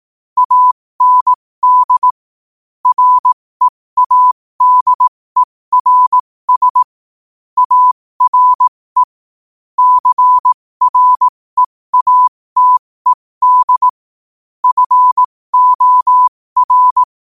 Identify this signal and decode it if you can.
Morse code, heard as a keyed tone, about 18 wpm, spaced as for 15 wpm, 1000 Hz. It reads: AND READERS ARE CREATED FOR
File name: qotdfiles / 15wpm